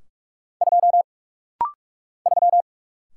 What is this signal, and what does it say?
Morse code, heard as a keyed tone, about 44 wpm, 690 Hz. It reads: 2 3
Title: Single Letter-Number - Rapid-Fire 45wpm